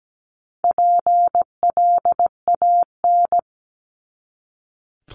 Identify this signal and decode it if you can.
Morse code, heard as a keyed tone, about 17 wpm, 700 Hz. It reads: PLAN
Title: ING Suffix - Top 100 Words with Base Words 17wpm